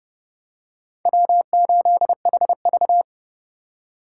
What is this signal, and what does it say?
Morse code, heard as a keyed tone, about 29 wpm, 700 Hz. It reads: W8HV